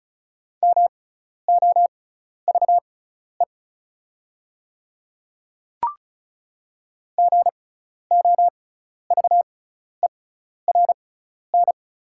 Morse code, heard as a keyed tone, about 36 wpm, spaced as for 12 wpm, 690 Hz. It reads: MOVE GOVERN
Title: 6x Character Spacing - Top 500 Words - 35wpm